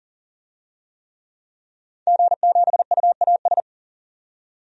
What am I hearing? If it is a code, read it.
Morse code, heard as a keyed tone, about 39 wpm, 700 Hz. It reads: G7UAS
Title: DXCC-100 Call Signs - 5 Characters - 40wpm